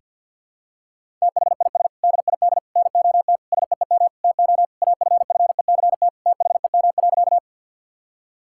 Morse code, heard as a keyed tone, about 45 wpm, 700 Hz. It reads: THIS DID NOT SEEM TO AFFECT THEM.
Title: Sentences from Top 1000 Words - Review Part 6 - 50wpm